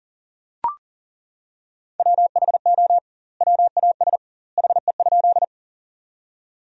Morse code, heard as a keyed tone, about 40 wpm, 700 Hz. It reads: WHO WAS HE?
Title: 2x Word-Spacing - Sentences from Top 100 Words - 40wpm